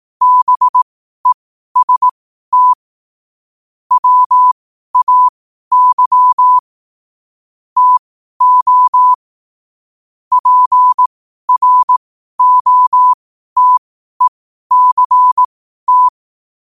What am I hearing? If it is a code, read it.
Morse code, heard as a keyed tone, about 18 wpm, spaced as for 12 wpm, 1000 Hz. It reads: BEST WAY TO PROTECT
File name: qotdfiles / 12wpm